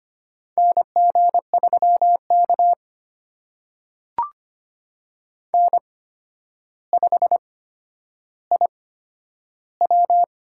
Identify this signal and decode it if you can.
Morse code, heard as a keyed tone, about 25 wpm, spaced as for 11 wpm, 700 Hz. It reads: NG3K N5IW
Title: USA Call Signs - 4 Characters - 8x Character Spacing - 25wpm